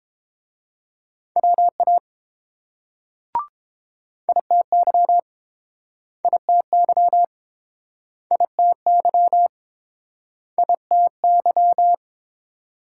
Morse code, heard as a keyed tone, about 30 wpm, 700 Hz. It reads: WA ITY ITY ITY ITY